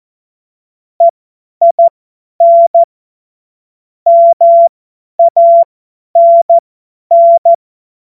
Morse code, heard as a keyed tone, about 14 wpm, spaced as for 10 wpm, 680 Hz. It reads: EIN MANN
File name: Morsekurs / Witze / Witz 18. Januar 2022, 50 F70